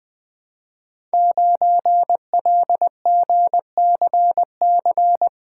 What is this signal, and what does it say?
Morse code, heard as a keyed tone, about 20 wpm, 700 Hz. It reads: 9LGCC
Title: Code Groups - 5 Letter:Numbers - 20wpm